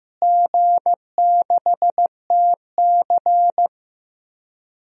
Morse code, heard as a keyed tone, about 15 wpm, 700 Hz. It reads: G6TC
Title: DXCC-100 Call Signs - 4 Characters - Repeated 2x - 15wpm